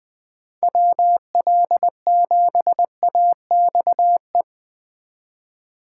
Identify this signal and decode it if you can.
Morse code, heard as a keyed tone, about 20 wpm, 700 Hz. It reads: WL7AXE
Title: Speed Racing - USA 2x3 Call Signs 17wpm